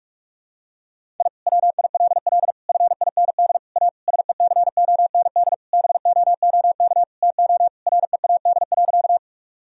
Morse code, heard as a keyed tone, about 46 wpm, 700 Hz. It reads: I WILL FIND A SECOND BOOK TO READ.
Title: Sentences from Top 1000 Words - Review Part 1 - 45wpm